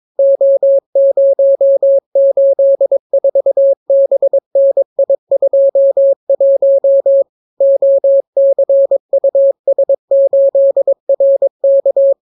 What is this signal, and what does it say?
Morse code, heard as a keyed tone, about 22 wpm, 550 Hz. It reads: O084BNI21 OCUS8RK